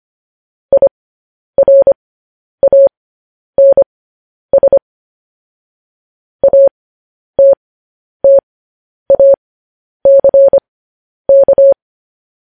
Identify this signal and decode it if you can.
Morse code, heard as a keyed tone, about 25 wpm, spaced as for 10 wpm, 550 Hz. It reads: IRANS ATTACK